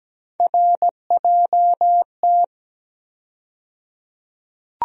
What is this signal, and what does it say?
Morse code, heard as a keyed tone, about 17 wpm, 700 Hz. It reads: RJT
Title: USA Call Sign Postfixes - 3 Letters 17wpm